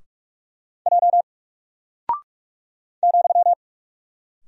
Morse code, heard as a keyed tone, about 44 wpm, 700 Hz. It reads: J ,
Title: Single Character (Letter:Number:Punctuation:Prosign) Alt 45wpm